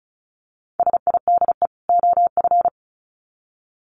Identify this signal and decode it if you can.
Morse code, heard as a keyed tone, about 34 wpm, 700 Hz. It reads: SIDE OF
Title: Sets of 2 Words 35wpm